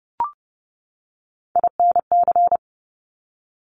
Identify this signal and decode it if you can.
Morse code, heard as a keyed tone, about 30 wpm, 700 Hz. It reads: INC
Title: Top Three Letter Combinations in Words 30wpm